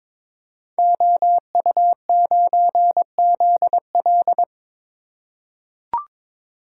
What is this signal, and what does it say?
Morse code, heard as a keyed tone, about 22 wpm, 700 Hz. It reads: OU9ZL
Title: Code Groups - 5 Letter:Numbers - 22wpm